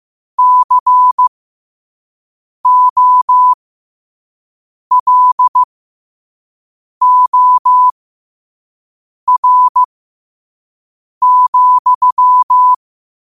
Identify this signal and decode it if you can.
Morse code, heard as a keyed tone, about 15 wpm, spaced as for 5 wpm, 1000 Hz. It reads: COLOR,
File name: qotdfiles / 5wpm